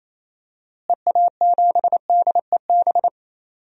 Morse code, heard as a keyed tone, about 28 wpm, 700 Hz. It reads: EA7DEB
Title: DXCC-100 Call Signs - 6 Characters - Repeated 2x - 28wpm